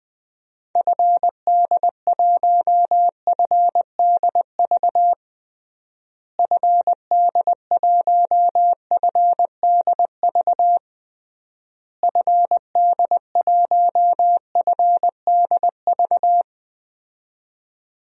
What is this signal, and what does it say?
Morse code, heard as a keyed tone, about 20 wpm, 700 Hz. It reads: FD1FDV FD1FDV FD1FDV
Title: DXCC-100 Call Signs - 6 Characters - Repeated 3x - 20wpm